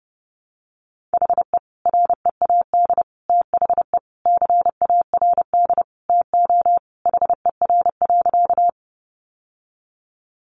Morse code, heard as a keyed tone, about 30 wpm, 700 Hz. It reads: HE READ THE CARD TO HER.
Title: Sentences from Top 600 Words 30wpm